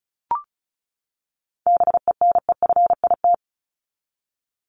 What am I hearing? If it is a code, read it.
Morse code, heard as a keyed tone, about 34 wpm, 700 Hz. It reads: BENEFIT
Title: Top 751-1000 Words 35wpm